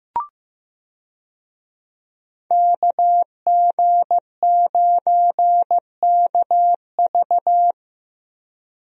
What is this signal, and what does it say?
Morse code, heard as a keyed tone, about 15 wpm, 700 Hz. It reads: KG9KV
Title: USA Call Signs - 5 Characters - Repeated 2x - 15wpm